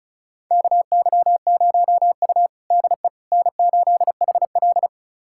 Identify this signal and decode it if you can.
Morse code, heard as a keyed tone, about 36 wpm, 700 Hz. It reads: KY0U DE N8HL
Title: USA Callsign DE Callsign - 4 Characters - 35wpm